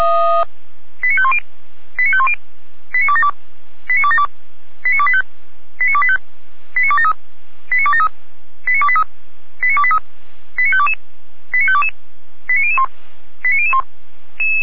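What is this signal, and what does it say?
Five-tone selective call (ZVEI), heard as quick runs of five tones, tones of about 70 ms. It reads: 89510 89510 89272 89272 89285 89285 89273 89273 89283 89283 89522 89522 89001 89001
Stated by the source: ZVEI five-tone call received by SDR.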